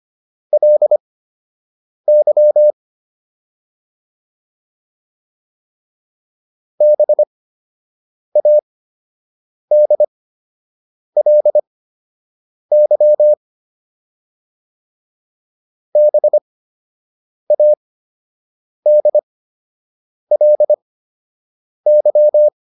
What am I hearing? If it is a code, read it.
Morse code, heard as a keyed tone, about 25 wpm, spaced as for 7 wpm, 600 Hz. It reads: LY BADLY BADLY